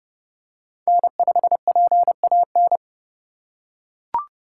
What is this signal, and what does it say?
Morse code, heard as a keyed tone, about 28 wpm, 700 Hz. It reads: N5PAN